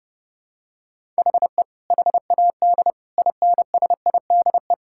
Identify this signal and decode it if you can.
Morse code, heard as a keyed tone, about 30 wpm, 700 Hz. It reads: HE HAD INSIDE
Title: Sentences from Top 1000 Words - Review Part 6 - 30wpm